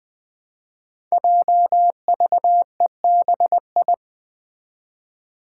Morse code, heard as a keyed tone, about 20 wpm, 700 Hz. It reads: JVEBI
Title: Five Challenging Letter-Number Combinations 20wpm